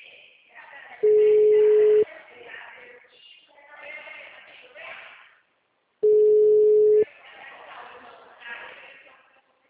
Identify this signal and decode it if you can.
Ringback tone (Europe, 425 Hz), with speech.